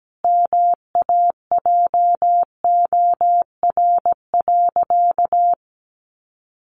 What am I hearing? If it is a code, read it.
Morse code, heard as a keyed tone, about 17 wpm, 700 Hz. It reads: MAJOR.
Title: Sentences from Top 600 Words 17wpm